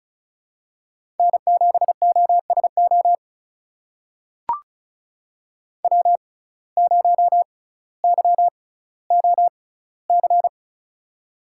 Morse code, heard as a keyed tone, about 36 wpm, spaced as for 18 wpm, 700 Hz. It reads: N7OSO W0YOC
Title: USA Call Signs - 5 Characters - 6x Character Spacing - 35wpm